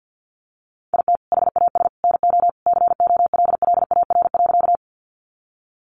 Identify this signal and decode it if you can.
Morse code, heard as a keyed tone, about 50 wpm, 700 Hz. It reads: IT HAS NO COLLAR.
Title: Easy Phrases 50wpm